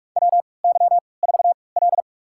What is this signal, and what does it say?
Morse code, heard as a keyed tone, about 46 wpm, spaced as for 26 wpm, 700 Hz. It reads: WY4L
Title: USA Call Signs - 4 Characters - 3x Character Spacing - 45wpm